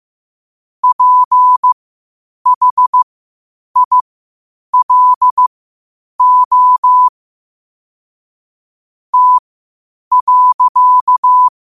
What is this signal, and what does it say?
Morse code, heard as a keyed tone, about 15 wpm, spaced as for 8 wpm, 1000 Hz. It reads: PHILO T.
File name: qotdfiles / 8wpm